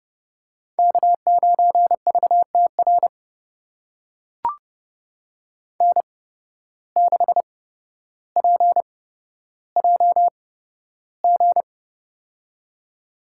Morse code, heard as a keyed tone, about 30 wpm, spaced as for 13 wpm, 700 Hz. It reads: K9VTR N6PJG
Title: USA Call Signs - 5 Characters - 8x Character Spacing - 30wpm